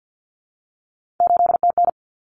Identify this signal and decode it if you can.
Morse code, heard as a keyed tone, about 48 wpm, 700 Hz. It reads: 8TN
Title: Common Cut Numbers - 3-Digit 50wpm